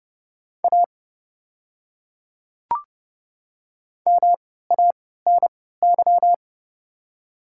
Morse code, heard as a keyed tone, about 29 wpm, spaced as for 17 wpm, 690 Hz. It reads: A MANY
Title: 3x Character Spacing - Top 100 Words - 30wpm